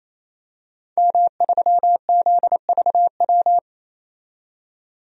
Morse code, heard as a keyed tone, about 28 wpm, 700 Hz. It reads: M3ZVW
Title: DXCC-100 Call Signs - 5 Characters - 28wpm